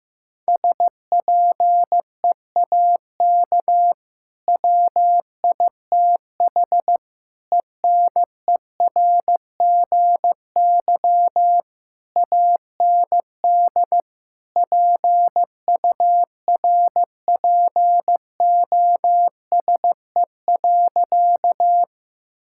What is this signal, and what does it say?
Morse code, heard as a keyed tone, about 16 wpm, 700 Hz. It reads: SPEAK WITH ENERGY AND PURPOSE.
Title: Sentences from Top 1000 Words - Review Part 6 - 15wpm